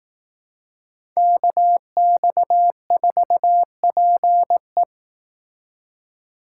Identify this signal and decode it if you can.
Morse code, heard as a keyed tone, about 18 wpm, 700 Hz. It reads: KX4PE
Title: Speed Racing - USA Call Signs 15wpm